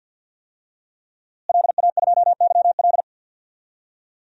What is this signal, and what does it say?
Morse code, heard as a keyed tone, about 52 wpm, 700 Hz. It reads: LA2YL
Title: DXCC-100 Call Signs - 5 Characters - Repeated 2x - 50wpm